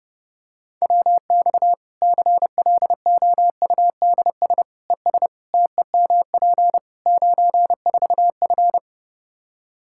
Morse code, heard as a keyed tone, about 32 wpm, 700 Hz. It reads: WX CLOUDS ES TEMP 94F
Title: Practice QSO - Exchange 2 - Info - 30wpm